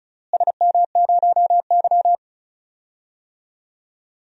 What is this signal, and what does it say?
Morse code, heard as a keyed tone, about 36 wpm, 700 Hz. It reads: SM0Y
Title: DXCC-100 Call Signs - 4 Characters - Repeated 2x - 35wpm